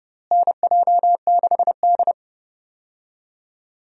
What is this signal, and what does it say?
Morse code, heard as a keyed tone, about 31 wpm, 700 Hz. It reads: NJ6D
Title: USA Call Signs - 4 Characters - Repeated 2x - 30wpm